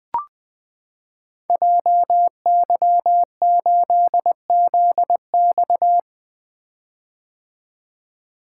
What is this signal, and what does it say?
Morse code, heard as a keyed tone, about 20 wpm, 700 Hz. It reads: JY8ZX
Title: Code Groups - 5 Letter:Numbers - 20wpm